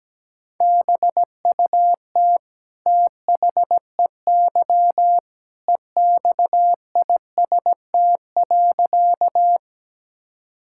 Morse code, heard as a keyed tone, about 17 wpm, 700 Hz. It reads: BUT THEY EXIST.
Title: Sentences from Top 1000 Words - Review Part 5 - 17wpm